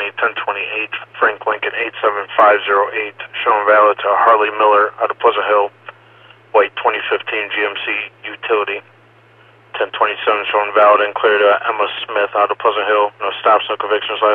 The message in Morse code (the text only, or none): none